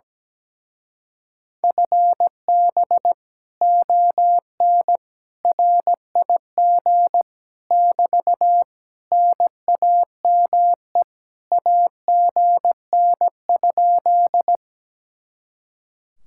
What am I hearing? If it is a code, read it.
Morse code, heard as a keyed tone, about 17 wpm, 700 Hz. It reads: FB ON RIG <BT> NAME AGN?
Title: Speed-Racing Common QSO Phrases - 17wpm